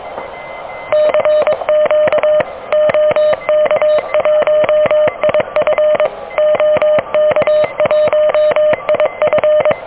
Morse code, heard as a keyed tone, about 23 wpm, 600 Hz. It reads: CQ OK1IF OK1IF
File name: OK QRP 2023 / OK QRP 2023